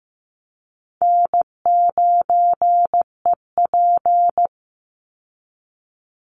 Morse code, heard as a keyed tone, about 15 wpm, 700 Hz. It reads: N9EP
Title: Easy Call Signs Repeated 2x - 15wpm